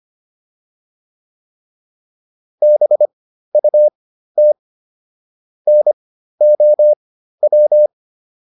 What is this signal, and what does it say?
Morse code, heard as a keyed tone, about 25 wpm, spaced as for 13 wpm, 600 Hz. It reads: BUT NOW